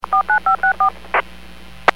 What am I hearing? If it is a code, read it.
DTMF, heard as telephone touch tones, tones of about 110 ms. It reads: 49564